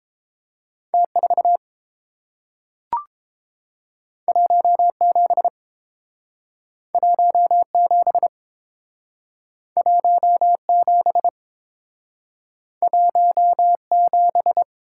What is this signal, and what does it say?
Morse code, heard as a keyed tone, about 30 wpm, 700 Hz. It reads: T 4 1 7 1 7 1 7 1 7